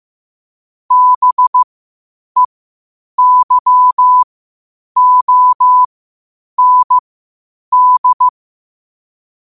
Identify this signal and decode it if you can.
Morse code, heard as a keyed tone, about 15 wpm, spaced as for 8 wpm, 1000 Hz. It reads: BEYOND
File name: qotdfiles / 8wpm